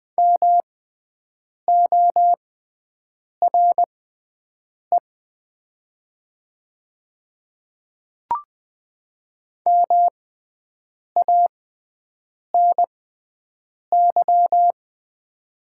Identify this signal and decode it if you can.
Morse code, heard as a keyed tone, about 20 wpm, spaced as for 7 wpm, 700 Hz. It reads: MORE MANY
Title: 6x Character Spacing - Top 100 Words - 20wpm